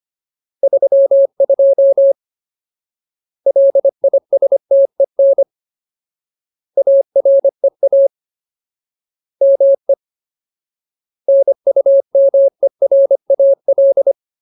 Morse code, heard as a keyed tone, about 25 wpm, 550 Hz. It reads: 32 LISTEN AREA ME NUMERAL
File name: Ultimate Word List / 25WPM Ultimate